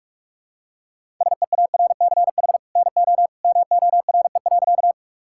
Morse code, heard as a keyed tone, about 41 wpm, 700 Hz. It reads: SEARCH NO MORE.
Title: Sentences from Top 1000 Words - Review Part 5 - 45wpm